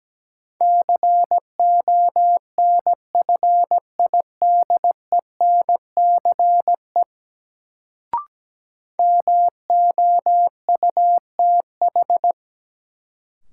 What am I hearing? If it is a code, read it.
Morse code, heard as a keyed tone, about 17 wpm, 700 Hz. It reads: CONFIDENCE MOUTH
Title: Sentences from Top 1500 Words - New Word Drill - 17wpm